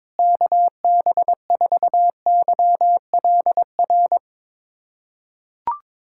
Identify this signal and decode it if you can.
Morse code, heard as a keyed tone, about 21 wpm, 700 Hz. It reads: KB4YLR
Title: USA Callsign DE Callsign - 6 Characters - 22wpm